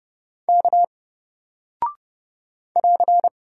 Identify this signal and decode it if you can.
Morse code, heard as a keyed tone, about 31 wpm, 700 Hz. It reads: K <AR>
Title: Single Character (Letter:Number:Punctuation:Prosign) Alt 30wpm